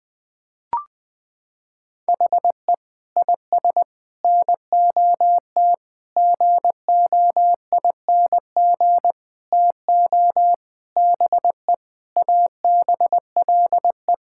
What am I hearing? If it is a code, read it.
Morse code, heard as a keyed tone, about 20 wpm, 700 Hz. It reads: HE IS NOT GOING TO BE ABLE
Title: Sentences with 6-Word Phrases - 20wpm